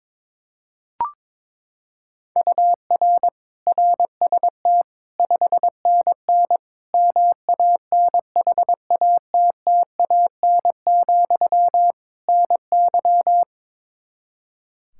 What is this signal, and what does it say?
Morse code, heard as a keyed tone, about 22 wpm, 700 Hz. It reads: UR RST 5NN MANHATTAN, NY
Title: Practice QSO - Exchange 1 - Beginning - 22wpm